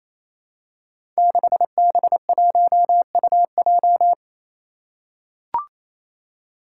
Morse code, heard as a keyed tone, about 27 wpm, 700 Hz. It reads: 6B1UJ
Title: Five Challenging Letter-Number Combinations 28wpm